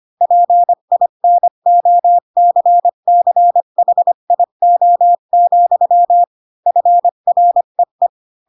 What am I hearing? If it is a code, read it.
Morse code, heard as a keyed tone, about 25 wpm, 700 Hz. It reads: PINOCCHIO, FREE
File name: Pinocchio / Pinocchio ch4 20 25 ik1hge